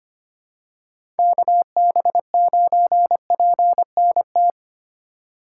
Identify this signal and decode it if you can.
Morse code, heard as a keyed tone, about 25 wpm, 700 Hz. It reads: KB9PNT
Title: USA Call Signs - 6 Characters - 3x Character Spacing - 25wpm